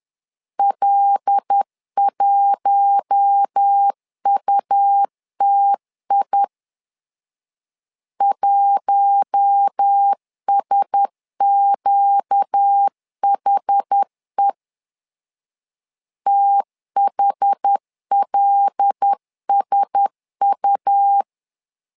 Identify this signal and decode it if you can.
Morse code, heard as a keyed tone, about 10 wpm, 800 Hz. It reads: L1UTI 1SQHE THLSU
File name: LEZIONI / 5 LBVC / 5 L 10wpm 2259